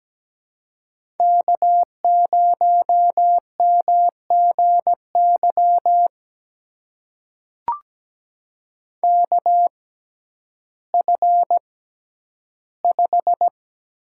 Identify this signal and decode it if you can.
Morse code, heard as a keyed tone, about 17 wpm, spaced as for 10 wpm, 700 Hz. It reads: K0MGY KF5
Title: USA Call Signs - 5 Characters - 6x Character Spacing - 17wpm